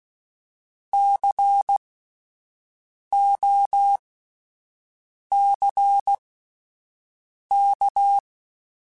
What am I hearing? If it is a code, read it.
Morse code, heard as a keyed tone, about 16 wpm, spaced as for 5 wpm, 800 Hz. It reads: COCK
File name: Spaced Alphas / Spaced Alphas mp3s / Cs sp @10